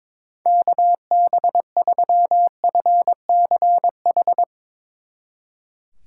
Morse code, heard as a keyed tone, about 22 wpm, 700 Hz. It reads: KB3FCH
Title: Speed Racing - USA 2x3 Call Signs 22wpm